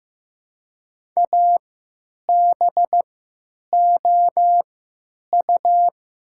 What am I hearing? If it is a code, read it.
Morse code, heard as a keyed tone, about 15 wpm, spaced as for 9 wpm, 700 Hz. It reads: ABOU
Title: 3x Character Spacing - Top 100 Words - 15wpm